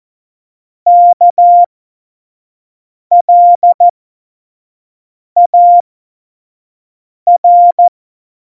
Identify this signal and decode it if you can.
Morse code, heard as a keyed tone, about 14 wpm, spaced as for 5 wpm, 700 Hz. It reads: KLAR
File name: Morsekurs / Witze / Witz 22. August 17, Tempo 25 F70